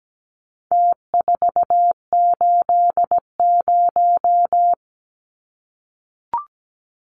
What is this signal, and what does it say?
Morse code, heard as a keyed tone, about 17 wpm, 700 Hz. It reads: T480
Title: Numbers - Contest Style - 4-Digit 17wpm